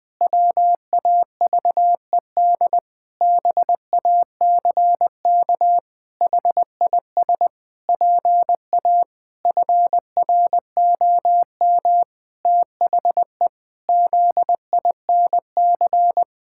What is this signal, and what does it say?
Morse code, heard as a keyed tone, about 21 wpm, 700 Hz. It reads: WAVED BACK HIS PA FROM THE ZINC
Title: Pangram Challenge v2 - 20wpm